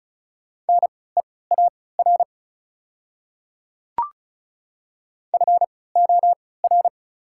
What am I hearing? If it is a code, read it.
Morse code, heard as a keyed tone, about 34 wpm, spaced as for 20 wpm, 690 Hz. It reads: NEAR FOR
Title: 3x Character Spacing - Top 200 Words - 35wpm